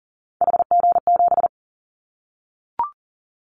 Morse code, heard as a keyed tone, about 40 wpm, 700 Hz. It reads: HG7
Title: Callsign Prefixes for Top 25 DXCC - Repeated 3x - 40wpm